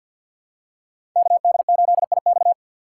Morse code, heard as a keyed tone, about 53 wpm, 700 Hz. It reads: KD8IX
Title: USA Call Signs - 5 Characters - 3x Character Spacing - 50wpm